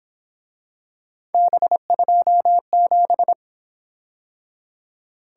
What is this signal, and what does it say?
Morse code, heard as a keyed tone, about 26 wpm, 700 Hz. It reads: B27
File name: Speed-Racing - Three Letter-Number Combinations - 22wpm